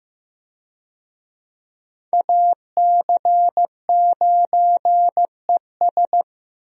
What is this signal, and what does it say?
Morse code, heard as a keyed tone, about 16 wpm, 700 Hz. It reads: AC9ES
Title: USA Call Signs - 5 Characters - 15wpm